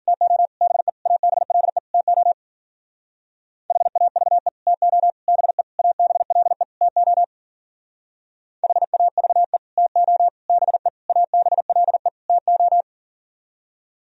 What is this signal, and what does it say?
Morse code, heard as a keyed tone, about 52 wpm, 700 Hz. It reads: TO BE ABLE TO HAVE TO BE ABLE TO HAVE TO BE ABLE TO